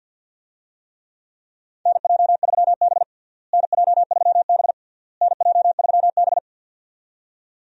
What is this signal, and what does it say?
Morse code, heard as a keyed tone, about 53 wpm, 700 Hz. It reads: NJ3B NJ3B NJ3B
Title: USA Call Signs - 4 Characters - Repeated 3x - 50wpm